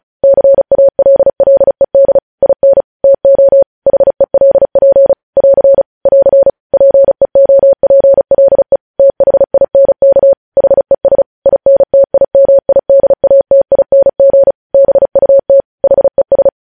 Morse code, heard as a keyed tone, about 36 wpm, 550 Hz. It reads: CALLED IN TO HELP <AR> <AR> PEOPLE THINK HES INTIMIDATING BUT HES
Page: news headlines in morse code at 35 wpm